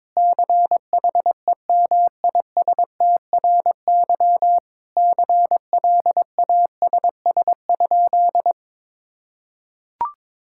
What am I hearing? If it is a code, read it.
Morse code, heard as a keyed tone, about 22 wpm, 700 Hz. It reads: CHEMISTRY CLASS?
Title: Sentences from Top 1000 Words - Review Part 6 - 22wpm